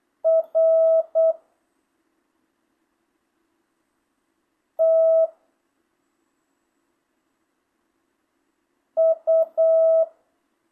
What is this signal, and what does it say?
Morse code, heard as a keyed tone, about 8 wpm, 660 Hz. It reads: R T U
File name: jh8cbh / 2025cw